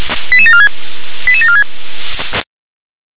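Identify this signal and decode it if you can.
Five-tone selective call (ZVEI), heard as quick runs of five tones, tones of about 70 ms. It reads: 88746 88746
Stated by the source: Five-tone ZVEI selective call received on an SDR.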